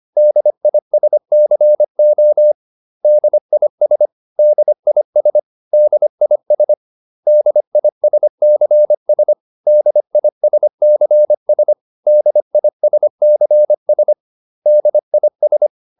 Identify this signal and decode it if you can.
Morse code, heard as a keyed tone, about 25 wpm, 600 Hz. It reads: DISCO DIS DIS DIS DISCS DISCS DISCS DIS